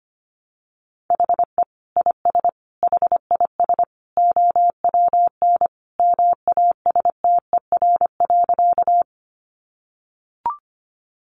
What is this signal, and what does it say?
Morse code, heard as a keyed tone, about 25 wpm, 700 Hz. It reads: HE IS HIS OWN MASTER.
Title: Sentences from Top 1300 Words 25wpm